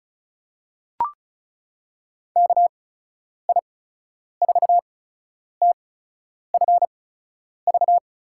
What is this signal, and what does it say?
Morse code, heard as a keyed tone, about 36 wpm, spaced as for 10 wpm, 700 Hz. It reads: KI4TFV
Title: USA Call Signs - 6 Characters - 8x Character Spacing - 35wpm